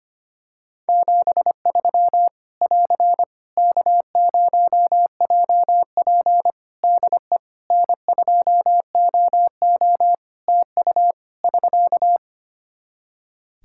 Morse code, heard as a keyed tone, about 25 wpm, 700 Hz. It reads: 73 <AR> K0JP DE N2OO TU <SK>
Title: Practice QSO - Exchange 4 - Ending - 25wpm